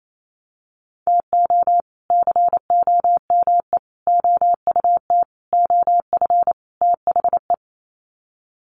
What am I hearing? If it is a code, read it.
Morse code, heard as a keyed tone, about 27 wpm, 700 Hz. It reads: TO COME OUT OF THE
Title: Sets of 5 Words Encore2 28wpm